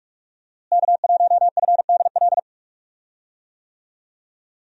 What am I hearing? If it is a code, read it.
Morse code, heard as a keyed tone, about 46 wpm, 700 Hz. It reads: K1FDL
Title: USA Call Signs - 5 Characters - Repeated 2x - 45wpm